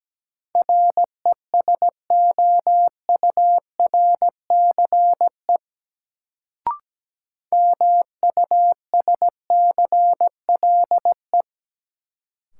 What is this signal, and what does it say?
Morse code, heard as a keyed tone, about 17 wpm, 700 Hz. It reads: RESOURCE MUSCLE
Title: Sentences from Top 1300 Words - New Word Drill - 17wpm